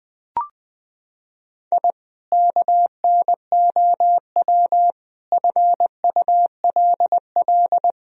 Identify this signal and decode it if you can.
Morse code, heard as a keyed tone, about 20 wpm, 700 Hz. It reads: I KNOW FULL
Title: Sentences from Top 1000 Words - Review Part 3 - 20wpm